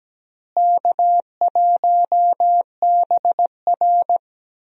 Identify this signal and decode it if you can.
Morse code, heard as a keyed tone, about 17 wpm, 700 Hz. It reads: K1BR
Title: USA Call Signs - 4 Characters - Repeated 3x - 17wpm